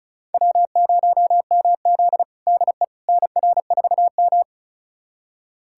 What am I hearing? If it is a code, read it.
Morse code, heard as a keyed tone, about 36 wpm, 700 Hz. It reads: W0MZ DE NR4M
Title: USA Callsign DE Callsign - 4 Characters - 35wpm